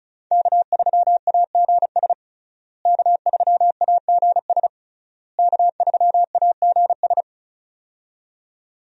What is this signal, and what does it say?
Morse code, heard as a keyed tone, about 36 wpm, 700 Hz. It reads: K3AGS K3AGS K3AGS
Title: USA Call Signs - 5 Characters - Repeated 3x - 35wpm